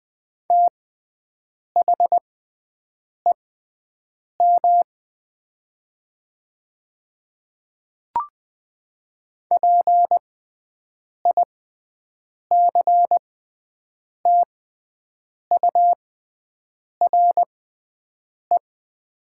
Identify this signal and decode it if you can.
Morse code, heard as a keyed tone, about 20 wpm, spaced as for 7 wpm, 700 Hz. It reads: THEM PICTURE
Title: 6x Character Spacing - Top 200 Words - 20wpm